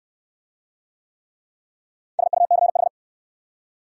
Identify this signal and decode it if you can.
Morse code, heard as a keyed tone, about 68 wpm, 700 Hz. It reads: SUCH